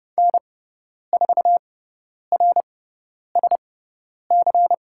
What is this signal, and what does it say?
Morse code, heard as a keyed tone, about 30 wpm, spaced as for 10 wpm, 700 Hz. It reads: N4RSC
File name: Lesson 16g - Letter c - Call Signs 30wpm